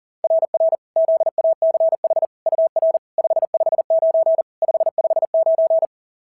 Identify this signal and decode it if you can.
Morse code, heard as a keyed tone, about 40 wpm, 690 Hz. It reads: RR ZACH UR 559 559
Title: SOTA QSO Practice 40wpm